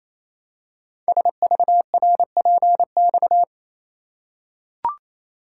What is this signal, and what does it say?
Morse code, heard as a keyed tone, about 27 wpm, 700 Hz. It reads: SVRPX
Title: Code Groups - 5 Letters - 28wpm